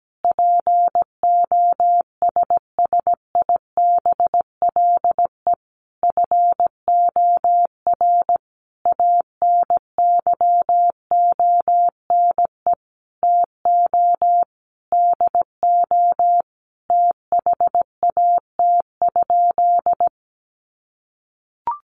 Morse code, heard as a keyed tone, about 17 wpm, 700 Hz. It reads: POSSIBLE FOR ANYONE TO DO THAT?
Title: Sentences from Top 900 Words 17wpm